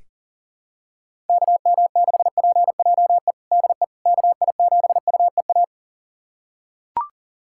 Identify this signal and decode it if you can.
Morse code, heard as a keyed tone, about 40 wpm, 700 Hz. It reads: KM6PJE DE KI7UEA
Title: USA Callsign DE Callsign 40wpm